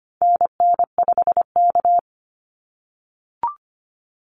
Morse code, heard as a keyed tone, about 25 wpm, 700 Hz. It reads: NN5K
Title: NA SOTA Activator Callsigns 25wpm